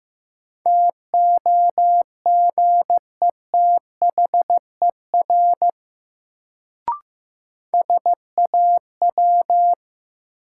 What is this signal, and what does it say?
Morse code, heard as a keyed tone, about 16 wpm, 700 Hz. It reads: TOGETHER SAW
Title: Sentences from Top 300 Words - New Word Drill - 15wpm